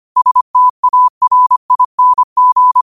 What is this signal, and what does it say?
Morse code, heard as a keyed tone, about 25 wpm, 1000 Hz. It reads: STARING
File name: qotdfiles / 25wpm